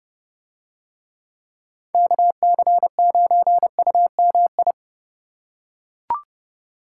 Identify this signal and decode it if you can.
Morse code, heard as a keyed tone, about 29 wpm, 700 Hz. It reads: KC9UMI